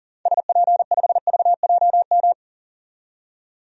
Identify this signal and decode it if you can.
Morse code, heard as a keyed tone, about 37 wpm, 700 Hz. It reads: SP5VJM
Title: DXCC-100 Call Signs - 6 Characters - 40wpm